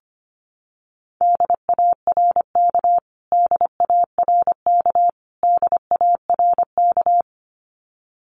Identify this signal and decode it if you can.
Morse code, heard as a keyed tone, about 25 wpm, 700 Hz. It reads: DARK DARK DARK
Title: Top 500 Words Repeated 25wpm